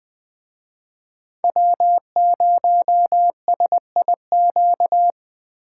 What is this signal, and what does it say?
Morse code, heard as a keyed tone, about 21 wpm, 700 Hz. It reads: W0SIQ